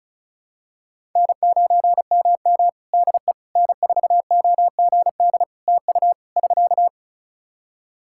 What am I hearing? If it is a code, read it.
Morse code, heard as a keyed tone, about 36 wpm, 700 Hz. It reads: N9MM DE N4OGD TU <SK>
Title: K1USN SST QSO Practice 35wpm